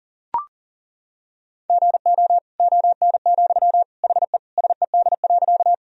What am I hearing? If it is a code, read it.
Morse code, heard as a keyed tone, about 40 wpm, 700 Hz. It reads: GO ON, HE SED.
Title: Sentences from Top 100 Words and Abbreviations 40wpm